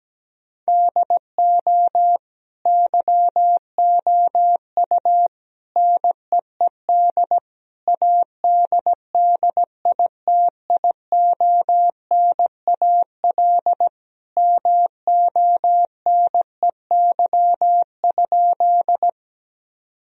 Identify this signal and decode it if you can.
Morse code, heard as a keyed tone, about 17 wpm, 700 Hz. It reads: DO YOU NEED ADDITIONAL MONEY?
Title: Sentences from Top 1000 Words - Review Part 4 - 17wpm